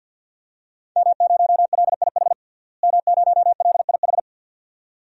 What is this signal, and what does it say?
Morse code, heard as a keyed tone, about 52 wpm, 700 Hz. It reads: M0LIH M0LIH
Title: DXCC-100 Call Signs - 5 Characters - Repeated 2x - 50wpm